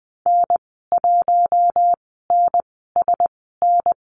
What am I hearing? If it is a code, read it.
Morse code, heard as a keyed tone, about 20 wpm, spaced as for 15 wpm, 700 Hz. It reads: N1NSN
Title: 048 Lesson 9g - Number 4 - Call Signs 20wpm